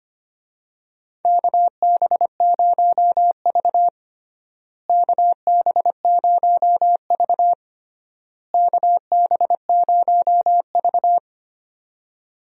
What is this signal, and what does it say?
Morse code, heard as a keyed tone, about 25 wpm, 700 Hz. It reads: KB0V KB0V KB0V
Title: USA Call Signs - 4 Characters - Repeated 3x - 25wpm